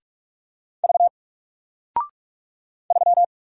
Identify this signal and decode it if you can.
Morse code, heard as a keyed tone, about 44 wpm, 700 Hz. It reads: V 3
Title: Single Character (Letter:Number:Punctuation:Prosign) 45wpm